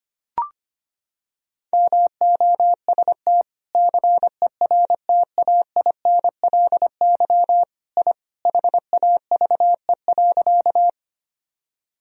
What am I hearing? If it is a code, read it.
Morse code, heard as a keyed tone, about 25 wpm, 700 Hz. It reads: MOST CERTAINLY I HAVE.
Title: Sentences from Top 1000 Words - Review Part 1 - 25wpm